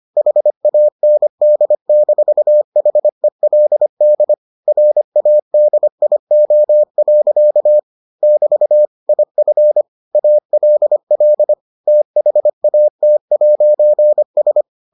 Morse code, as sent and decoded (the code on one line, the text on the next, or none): .... .- -. -.. -....- .... .-.. -.. / .-. .- -.. .. --- .-.-.- / -...- / .. ..-. / .- .-.. .-.. / - .... .- - .----. ...
HAND-HELD RADIO. <BT> IF ALL THAT'S